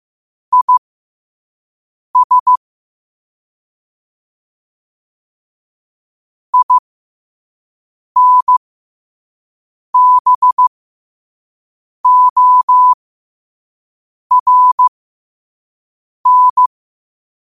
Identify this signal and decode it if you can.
Morse code, heard as a keyed tone, about 15 wpm, spaced as for 5 wpm, 1000 Hz. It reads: IS INBORN
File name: qotdfiles / 5wpm